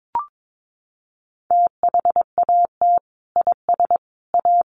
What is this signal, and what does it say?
Morse code, heard as a keyed tone, about 22 wpm, 700 Hz. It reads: THAT IS A
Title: Sentences from Top 700 Words 22wpm